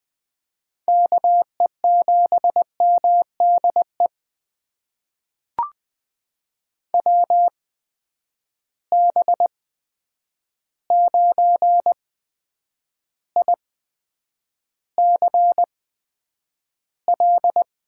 Morse code, heard as a keyed tone, about 20 wpm, spaced as for 9 wpm, 700 Hz. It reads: KE7MDE WB9ICL
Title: USA Call Signs - 6 Characters - 8x Character Spacing - 20wpm